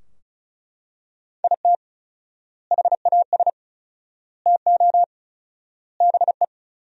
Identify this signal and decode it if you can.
Morse code, heard as a keyed tone, about 36 wpm, 700 Hz. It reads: IT HAS TO BE
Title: 4x Word-Spacing - Sentences from Top 500 Words - 35wpm